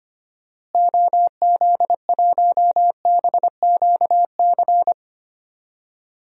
Morse code, heard as a keyed tone, about 25 wpm, 700 Hz. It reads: OZ1BQC
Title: DXCC-100 Call Signs - 6 Characters - Repeated 2x - 25wpm